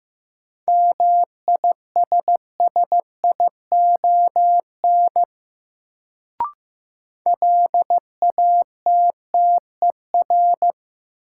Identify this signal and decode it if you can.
Morse code, heard as a keyed tone, about 15 wpm, 700 Hz. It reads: MISSION LATTER
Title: Sentences from Top 1600 Words - New Word Drill - 15wpm